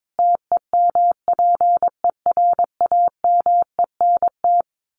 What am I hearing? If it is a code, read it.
Morse code, heard as a keyed tone, about 22 wpm, 700 Hz. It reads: TEMPERAMENT
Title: A Sound Of Thunder 22wpm